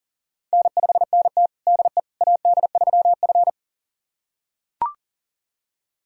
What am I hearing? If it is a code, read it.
Morse code, heard as a keyed tone, about 40 wpm, 700 Hz. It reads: N5NT DE AD3F
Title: USA Callsign DE Callsign - 4 Characters - 40wpm